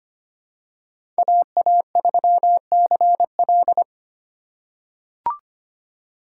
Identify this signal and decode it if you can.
Morse code, heard as a keyed tone, about 25 wpm, 700 Hz. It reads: AA3CL
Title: USA Call Signs - 5 Characters - 8x Character Spacing - 25wpm